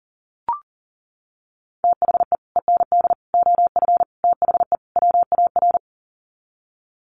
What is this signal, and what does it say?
Morse code, heard as a keyed tone, about 40 wpm, 700 Hz. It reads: THE END OF THE WAR
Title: Sets of 5 Words 40wpm